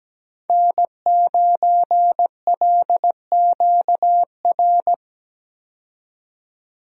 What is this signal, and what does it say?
Morse code, heard as a keyed tone, about 17 wpm, 700 Hz. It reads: N9LQR
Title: USA Call Signs - 5 Characters - 17wpm